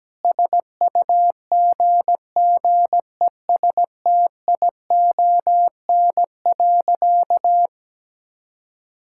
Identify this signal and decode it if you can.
Morse code, heard as a keyed tone, about 17 wpm, 700 Hz. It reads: SUGGESTION.
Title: Sentences from Top 1600 Words 17wpm